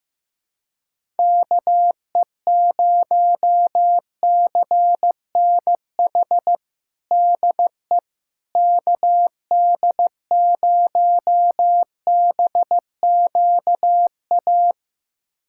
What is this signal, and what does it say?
Morse code, heard as a keyed tone, about 15 wpm, 700 Hz. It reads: KE0CNH DE KD0BQA
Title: USA Callsign DE Callsign - 6 Characters - 15wpm